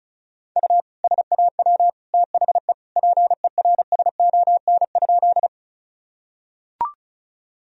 Morse code, heard as a keyed tone, about 36 wpm, 700 Hz. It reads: U SAW THE PERSON?
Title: Sentences from Top 400 Words and Abbreviations 35wpm